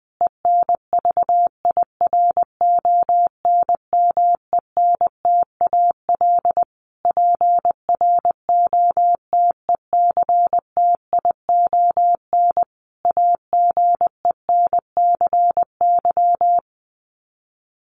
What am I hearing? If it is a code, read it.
Morse code, heard as a keyed tone, about 20 wpm, 700 Hz. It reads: ENVIRONMENTAL PROTECTION AGENCY
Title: Sets of 5 Words Encore2 20wpm